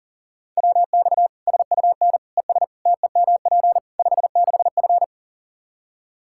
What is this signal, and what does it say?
Morse code, heard as a keyed tone, about 40 wpm, 700 Hz. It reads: WX SUN ES TEMP 56F
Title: Practice QSO - Exchange 2 - Info - 40wpm